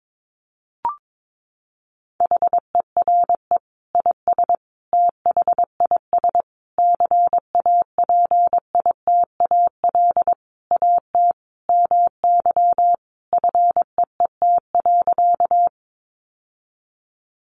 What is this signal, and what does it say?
Morse code, heard as a keyed tone, about 22 wpm, 700 Hz. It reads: HERE IS THIS CAPITAL AT MY FEET.
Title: Sentences from Top 1000 Words 22wpm